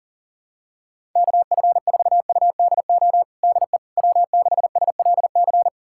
Morse code, heard as a keyed tone, about 37 wpm, 700 Hz. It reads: KF4UDO DE W6SLC
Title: USA Callsign DE Callsign 40wpm